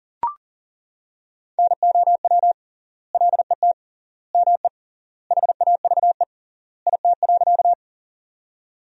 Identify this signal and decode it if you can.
Morse code, heard as a keyed tone, about 40 wpm, 700 Hz. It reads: NOW LET ME HAVE IT.
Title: 3x Word-Spacing - Sentences from Top 300 Words - 40wpm